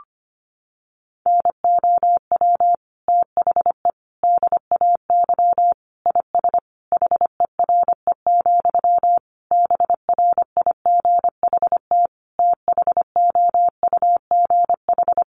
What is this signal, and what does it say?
Morse code, heard as a keyed tone, about 25 wpm, 700 Hz. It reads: NOW THE DAY IS HERE, BRIGHT THOUGH
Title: Sentences from Top 1000 Words 25wpm